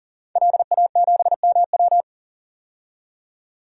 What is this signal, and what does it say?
Morse code, heard as a keyed tone, about 40 wpm, 700 Hz. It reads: LA7MW
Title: DXCC-100 Call Signs - 5 Characters - 40wpm